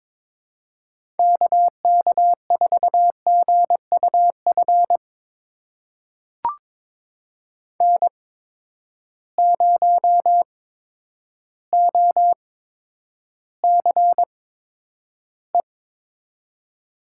Morse code, heard as a keyed tone, about 22 wpm, spaced as for 10 wpm, 700 Hz. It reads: KK4GUF N0OCE
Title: USA Call Signs - 8x Character Spacing - 22wpm